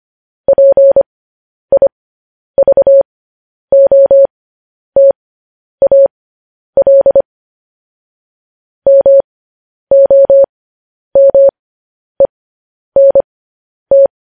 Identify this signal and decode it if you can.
Morse code, heard as a keyed tone, about 25 wpm, spaced as for 10 wpm, 550 Hz. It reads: PIVOTAL MOMENT
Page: news headlines in morse code at 10 wpm